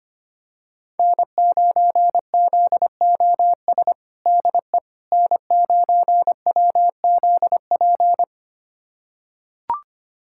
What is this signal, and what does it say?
Morse code, heard as a keyed tone, about 25 wpm, 700 Hz. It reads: N9ZOS DE N9WZP
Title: USA Callsign DE Callsign - 5 Characters - 25wpm